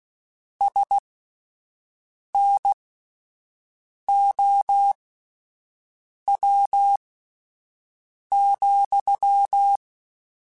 Morse code, heard as a keyed tone, about 16 wpm, spaced as for 5 wpm, 800 Hz. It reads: SNOW,
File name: (spaced) / QSOs Spaced-mp3s / WA7QOX 1 @10 sp